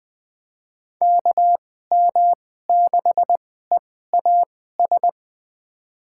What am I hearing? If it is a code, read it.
Morse code, heard as a keyed tone, about 21 wpm, spaced as for 15 wpm, 700 Hz. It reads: KM6EAS
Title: USA Call Signs - 6 Characters - 2x Character Spacing - 20wpm